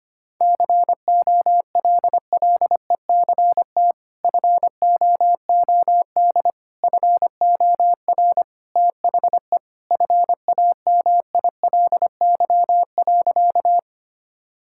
Morse code, heard as a keyed tone, about 25 wpm, 700 Hz. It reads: COLLECT FOOD FOR THE FAMILY.
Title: Sentences from Top 1600 Words 25wpm